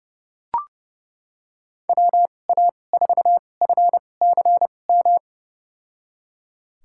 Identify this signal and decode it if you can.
Morse code, heard as a keyed tone, about 30 wpm, spaced as for 22 wpm, 700 Hz. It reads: WA4FCM
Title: USA Call Signs - 6 Characters - 2x Character Spacing - 30wpm